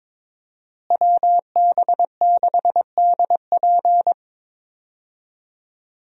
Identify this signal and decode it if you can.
Morse code, heard as a keyed tone, about 22 wpm, 700 Hz. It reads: WB6DP
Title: USA Call Signs - 5 Characters - Repeated 2x - 22wpm